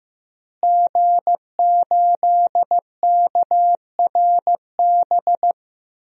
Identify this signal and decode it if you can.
Morse code, heard as a keyed tone, about 15 wpm, 700 Hz. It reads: G8KRB
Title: DXCC-100 Call Signs - 5 Characters - Repeated 2x - 15wpm